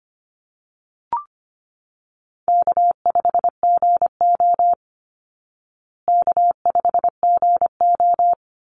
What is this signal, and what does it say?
Morse code, heard as a keyed tone, about 25 wpm, 700 Hz. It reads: K5GO K5GO
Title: Easy Call Signs Repeated 3x - 25wpm